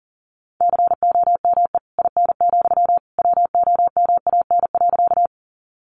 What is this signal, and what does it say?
Morse code, heard as a keyed tone, about 37 wpm, 700 Hz. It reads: COME IN, WOMAN.
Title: Sentences from Top 1100 Words 40wpm